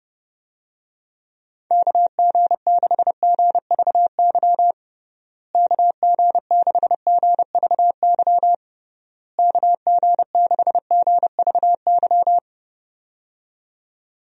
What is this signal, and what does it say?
Morse code, heard as a keyed tone, about 30 wpm, 700 Hz. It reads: KG6GVY KG6GVY KG6GVY
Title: USA Call Signs - 6 Characters - Repeated 3x - 30wpm